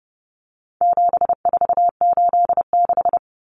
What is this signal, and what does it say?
Morse code, heard as a keyed tone, about 30 wpm, 700 Hz. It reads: 7486
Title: Common Cut Numbers - 4-Digit 30wpm